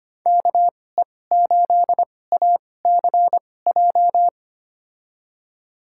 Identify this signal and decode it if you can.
Morse code, heard as a keyed tone, about 25 wpm, spaced as for 18 wpm, 700 Hz. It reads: KE8ACJ
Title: USA Call Signs - 6 Characters - 2x Character Spacing - 25wpm